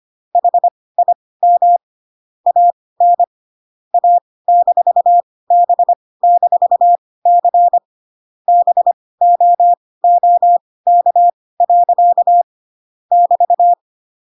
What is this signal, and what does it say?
Morse code, heard as a keyed tone, about 25 wpm, spaced as for 18 wpm, 700 Hz. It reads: HIM AN A-B-C BOOK. <BT>